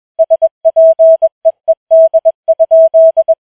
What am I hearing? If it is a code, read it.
Morse code, heard as a keyed tone, about 21 wpm, 650 Hz. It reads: SPEED?
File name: challenge-184